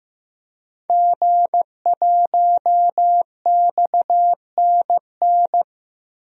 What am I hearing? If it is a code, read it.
Morse code, heard as a keyed tone, about 15 wpm, 700 Hz. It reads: G1XNN
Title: DXCC-100 Call Signs - 5 Characters - Repeated 2x - 15wpm